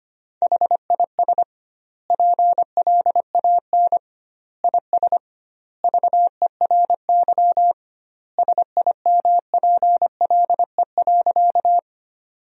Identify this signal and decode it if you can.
Morse code, heard as a keyed tone, about 25 wpm, 700 Hz. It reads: HIS PLAN IS VERY SIMPLE.
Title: 2x Word-Spacing - Sentences from Top 500 Words - 25wpm